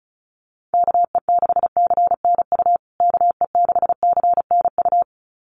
Morse code, heard as a keyed tone, about 34 wpm, 700 Hz. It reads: KE6CNU KE6CNU
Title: USA Call Signs Repeated 2x 35wpm